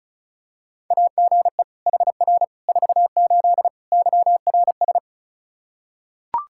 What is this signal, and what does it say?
Morse code, heard as a keyed tone, about 36 wpm, 700 Hz. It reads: AGE HR 48 YRS
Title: Practice QSO - Exchange 3 - More Info - 35wpm